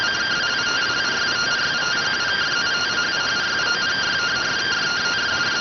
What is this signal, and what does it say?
Two-tone teleprinter signal, about 170 Hz shift.